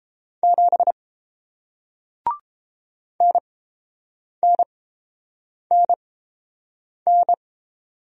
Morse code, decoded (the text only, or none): 7 NNNN